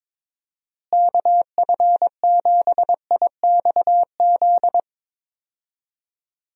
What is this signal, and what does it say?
Morse code, heard as a keyed tone, about 22 wpm, 700 Hz. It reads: KF7IXZ
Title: USA Call Signs - 6 Characters - Repeated 2x - 22wpm